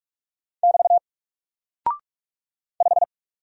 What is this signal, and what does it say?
Morse code, heard as a keyed tone, about 44 wpm, 700 Hz. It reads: <BT> 5
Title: Single Character (Letter:Number:Punctuation:Prosign) Alt 45wpm